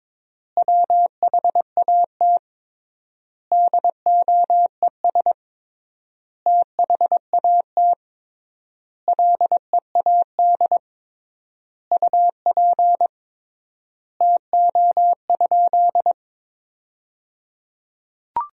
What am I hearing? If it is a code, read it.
Morse code, heard as a keyed tone, about 21 wpm, 700 Hz. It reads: WHAT DOES THAT LEAD UP TO?
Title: 3x Word-Spacing - Sentences from Top 500 Words - 22wpm